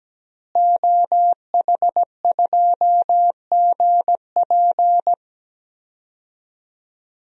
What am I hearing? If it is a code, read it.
Morse code, heard as a keyed tone, about 17 wpm, 700 Hz. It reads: OH2GP
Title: DXCC-100 Call Signs - 5 Characters - Repeated 2x - 17wpm